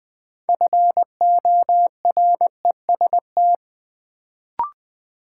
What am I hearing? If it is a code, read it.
Morse code, heard as a keyed tone, about 20 wpm, 700 Hz. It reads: FOREST
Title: Build A Word - 2 Letters At A Time - 6-Letter Words - 20wpm